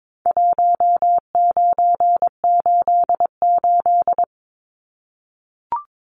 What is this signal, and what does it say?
Morse code, heard as a keyed tone, about 22 wpm, 700 Hz. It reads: 1988
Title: Numbers - Contest Style - 4-Digit 22wpm